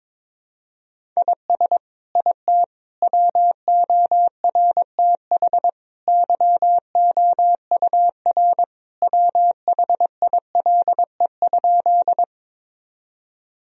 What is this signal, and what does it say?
Morse code, heard as a keyed tone, about 21 wpm, 700 Hz. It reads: IS IT WORTH YOUR WHILE?
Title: Sentences from Top 1000 Words - Review Part 1 - 22wpm